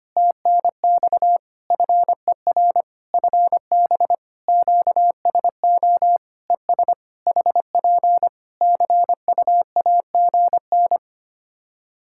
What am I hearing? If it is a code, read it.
Morse code, heard as a keyed tone, about 25 wpm, 700 Hz. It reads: TNX FER FB QSO ES HP CUAGN
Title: Practice QSO - Exchange 4 - Ending - 25wpm